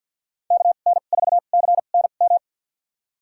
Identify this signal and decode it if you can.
Morse code, heard as a keyed tone, about 51 wpm, spaced as for 36 wpm, 700 Hz. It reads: KN4CNM
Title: USA Call Signs - 2x Character Spacing - 50wpm